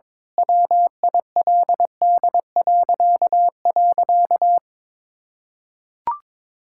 Morse code, heard as a keyed tone, about 22 wpm, 700 Hz. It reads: WILD..
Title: Pangram Challenge v2 - 22wpm